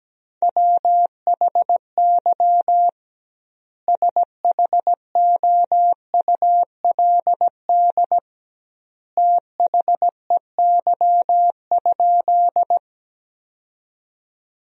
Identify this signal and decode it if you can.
Morse code, heard as a keyed tone, about 17 wpm, 700 Hz. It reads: WHY SHOULD THEY?
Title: 2x Word-Spacing - Sentences from Top 300 Words - 17wpm